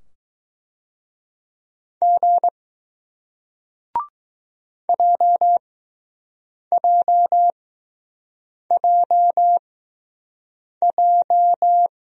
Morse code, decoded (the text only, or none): G JJJJ